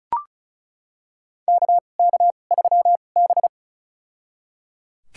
Morse code, heard as a keyed tone, about 34 wpm, spaced as for 25 wpm, 700 Hz. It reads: KK3B
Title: USA Call Signs - 4 Characters - 2x Character Spacing - 35wpm